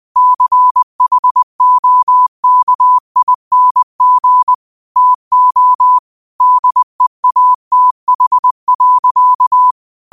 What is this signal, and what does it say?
Morse code, heard as a keyed tone, about 20 wpm, 1000 Hz. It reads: CHOKING TO DEATH.
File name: 20wpm